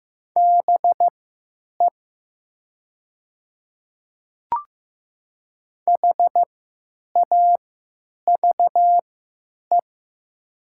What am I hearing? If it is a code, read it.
Morse code, heard as a keyed tone, about 15 wpm, spaced as for 9 wpm, 700 Hz. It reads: BE HAVE
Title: 3x Character Spacing - Top 100 Words - 15wpm